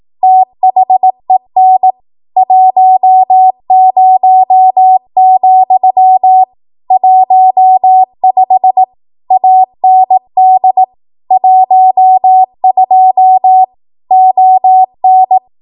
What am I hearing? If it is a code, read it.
Morse code, heard as a keyed tone, about 18 wpm, 750 Hz. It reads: THEN 10, 15 AND 12 ON
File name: Archive / 18 WPM / 18WPM